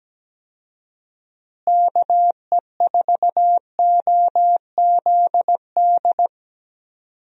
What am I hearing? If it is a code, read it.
Morse code, heard as a keyed tone, about 17 wpm, 700 Hz. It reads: KE4OZD
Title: USA Call Signs - 6 Characters - Repeated 3x - 17wpm